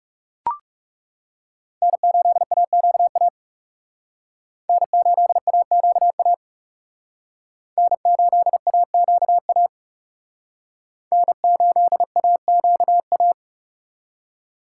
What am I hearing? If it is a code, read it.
Morse code, heard as a keyed tone, about 37 wpm, 700 Hz. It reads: N8AQA N8AQA N8AQA N8AQA
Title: Speed Racing - USA 1x3 Call Signs 30wpm